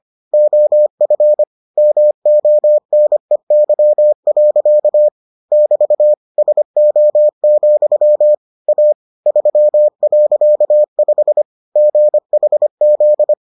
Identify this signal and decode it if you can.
Morse code, heard as a keyed tone, about 25 wpm, 600 Hz. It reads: OF MONEY. <BT> SO, A 3.5 GHZ